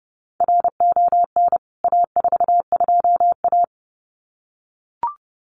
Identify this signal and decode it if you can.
Morse code, heard as a keyed tone, about 30 wpm, 700 Hz. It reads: RON A42A
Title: CWT Exchange 30wpm